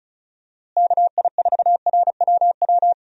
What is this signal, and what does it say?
Morse code, heard as a keyed tone, about 36 wpm, 700 Hz. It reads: KI4RWW
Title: USA Call Signs - 6 Characters - Repeated 2x - 35wpm